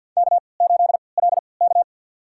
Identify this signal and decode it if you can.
Morse code, heard as a keyed tone, about 51 wpm, spaced as for 28 wpm, 700 Hz. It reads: K8LK
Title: USA Call Signs - 4 Characters - 3x Character Spacing - 50wpm